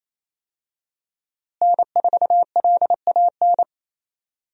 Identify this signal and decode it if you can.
Morse code, heard as a keyed tone, about 28 wpm, 700 Hz. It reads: N4LAN
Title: USA Call Signs - 5 Characters - Repeated 2x - 28wpm